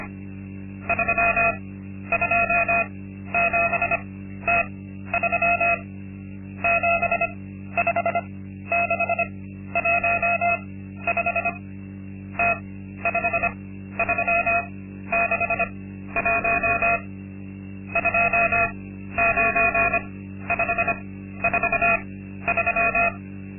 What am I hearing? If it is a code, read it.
Morse code, heard as a keyed tone, about 25 wpm, spaced as for 13 wpm, 680 Hz. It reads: 327T3 75615 T5361 29543
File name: Linksmakalnis-2